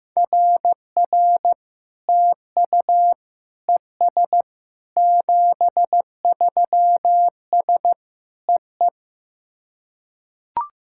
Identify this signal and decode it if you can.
Morse code, heard as a keyed tone, about 15 wpm, 700 Hz. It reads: RR TU ES 73S EE
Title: SOTA QSO Practice 15wpm